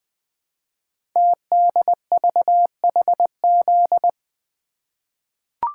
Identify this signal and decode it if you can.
Morse code, heard as a keyed tone, about 20 wpm, 700 Hz. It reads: TDVHZ
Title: Code Groups - 5 Letters - 20wpm